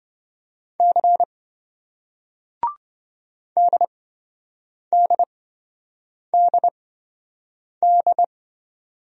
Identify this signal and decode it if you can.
Morse code, heard as a keyed tone, about 28 wpm, spaced as for 7 wpm, 700 Hz. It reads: C DDDD